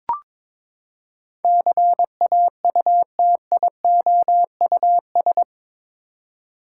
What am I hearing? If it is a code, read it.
Morse code, heard as a keyed tone, about 22 wpm, 700 Hz. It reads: CAUTIOUS
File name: practice / 22wpm / Eight Letter Words - 22wpm